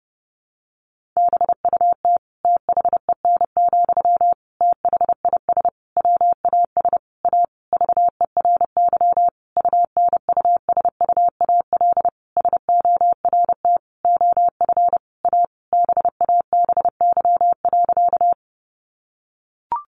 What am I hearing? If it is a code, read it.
Morse code, heard as a keyed tone, about 28 wpm, 700 Hz. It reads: BUT THEN, THIS WAS A VERY UNUSUAL SORT OF A BABY.
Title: Sentences from Top 1400 Words 30wpm